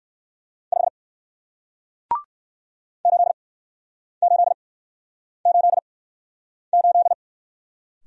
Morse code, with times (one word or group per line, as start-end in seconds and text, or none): 0.7-0.9 5
3.0-7.1 8888